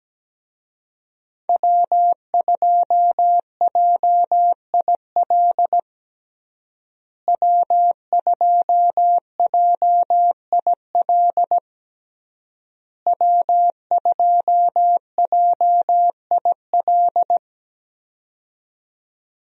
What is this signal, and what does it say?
Morse code, heard as a keyed tone, about 17 wpm, 700 Hz. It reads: W2JIL W2JIL W2JIL
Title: USA Call Signs - 5 Characters - Repeated 3x - 17wpm